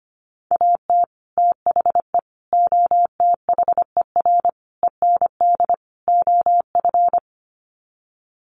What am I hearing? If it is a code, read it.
Morse code, heard as a keyed tone, about 25 wpm, 700 Hz. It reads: AT THE OTHER END OF
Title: Sets of 5 Words 25wpm